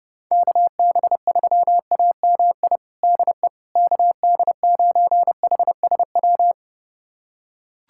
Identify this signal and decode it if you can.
Morse code, heard as a keyed tone, about 29 wpm, 700 Hz. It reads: KB3AMI DE KD9HSW
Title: USA Callsign DE Callsign - 6 Characters - 30wpm